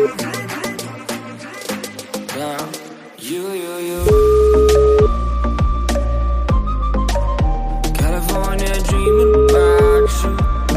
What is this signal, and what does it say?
Ringback tone (Europe, 425 Hz), with noise.